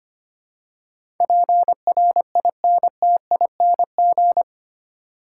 Morse code, heard as a keyed tone, about 25 wpm, 700 Hz. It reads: PRINTING
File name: Eight Letter Words - 25wpm